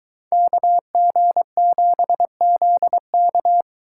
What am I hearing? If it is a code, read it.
Morse code, heard as a keyed tone, about 23 wpm, 700 Hz. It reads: KG7ZK